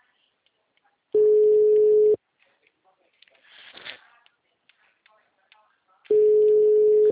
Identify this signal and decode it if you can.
Ringback tone (Europe, 425 Hz), with speech.